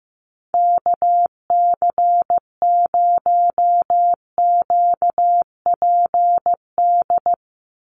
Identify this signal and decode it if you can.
Morse code, heard as a keyed tone, about 15 wpm, 700 Hz. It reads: KC0QPD
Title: USA Call Signs 15wpm